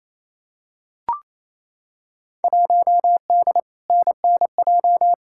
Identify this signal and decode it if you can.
Morse code, heard as a keyed tone, about 28 wpm, 700 Hz. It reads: 1D NNJ
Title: Field Day Exchanges Encore - 28wpm